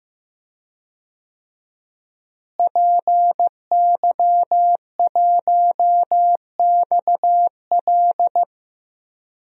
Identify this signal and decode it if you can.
Morse code, heard as a keyed tone, about 15 wpm, 700 Hz. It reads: PY1XL
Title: DXCC-100 Call Signs - 5 Characters - Repeated 2x - 15wpm